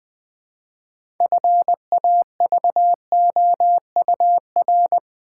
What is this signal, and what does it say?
Morse code, heard as a keyed tone, about 20 wpm, 700 Hz. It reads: FAVOUR
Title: Build A Word - 2 Letters At A Time - 6-Letter Words - 20wpm